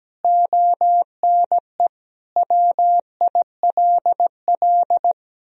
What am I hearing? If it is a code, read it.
Morse code, heard as a keyed tone, about 17 wpm, 700 Hz. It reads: ONE WILL
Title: Sentences from Top 1000 Words - Review Part 2 - 17wpm